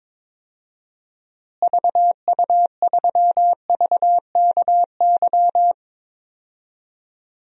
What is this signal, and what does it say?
Morse code, heard as a keyed tone, about 22 wpm, 700 Hz. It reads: VU3VKY
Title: DXCC-100 Call Signs - 6 Characters - Repeated 3x - 22wpm